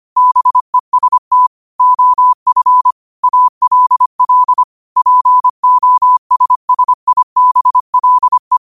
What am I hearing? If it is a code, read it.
Morse code, heard as a keyed tone, about 25 wpm, 1000 Hz. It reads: BEST OF ALL POSSIBLE
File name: qotdfiles / 25wpm